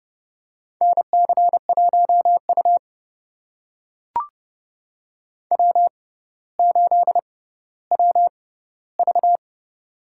Morse code, heard as a keyed tone, about 30 wpm, spaced as for 15 wpm, 700 Hz. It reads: NC1U W8WV
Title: USA Call Signs - 4 Characters - 6x Character Spacing - 30wpm